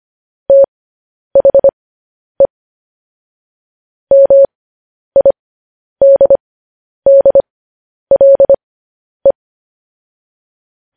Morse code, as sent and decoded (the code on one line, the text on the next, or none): - .... / -- .. -.. -.. .-.. .
THE MIDDLE